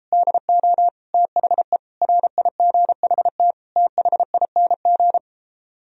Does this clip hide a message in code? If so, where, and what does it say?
Yes: Morse code from 0.1 to 5.2 s, heard as a keyed tone, about 38 wpm, 700 Hz. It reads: DO THE RIGHT THING